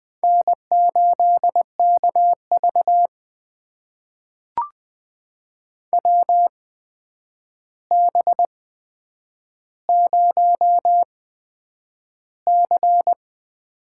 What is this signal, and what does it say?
Morse code, heard as a keyed tone, about 20 wpm, spaced as for 9 wpm, 700 Hz. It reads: N8KV WB0C
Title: USA Call Signs - 4 Characters - 8x Character Spacing - 20wpm